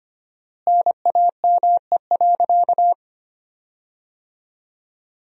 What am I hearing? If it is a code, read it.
Morse code, heard as a keyed tone, about 25 wpm, 700 Hz. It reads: NAME.
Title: 4x Word-Spacing - Sentences from Top 500 Words - 25wpm